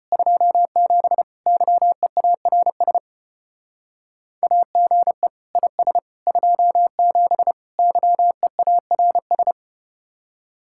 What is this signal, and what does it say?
Morse code, heard as a keyed tone, about 34 wpm, 700 Hz. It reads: 27 YEARS AGE IS 27 YEARS